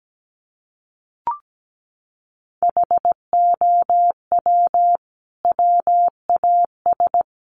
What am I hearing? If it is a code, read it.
Morse code, heard as a keyed tone, about 17 wpm, 700 Hz. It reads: HOW WAS
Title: Sentences from Top 100 Words 17wpm